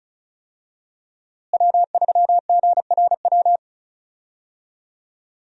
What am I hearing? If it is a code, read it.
Morse code, heard as a keyed tone, about 36 wpm, 700 Hz. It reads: W3GRW
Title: USA Call Signs - 5 Characters - Repeated 2x - 35wpm